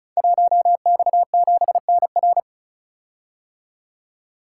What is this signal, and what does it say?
Morse code, heard as a keyed tone, about 36 wpm, 700 Hz. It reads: 1X7NR
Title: Code Groups - 5 Letter:Numbers - 35wpm